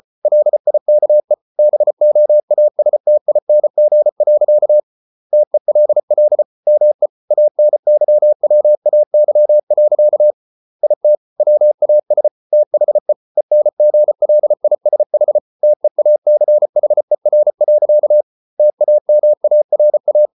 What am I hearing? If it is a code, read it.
Morse code, heard as a keyed tone, about 34 wpm, 600 Hz. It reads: LIKE BOASTING. TELL ME ANYWAY. IT WAS THE ENGLISH TEACHER. TAMARA